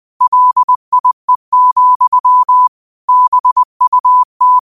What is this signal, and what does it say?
Morse code, heard as a keyed tone, about 20 wpm, 1000 Hz. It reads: LIE, BUT
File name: qotdfiles / 20wpm